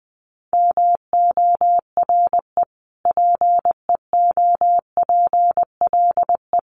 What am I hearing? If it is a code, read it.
Morse code, heard as a keyed tone, about 20 wpm, 700 Hz. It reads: MORE PEOPLE
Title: Sentences from Top 1500 Words 20wpm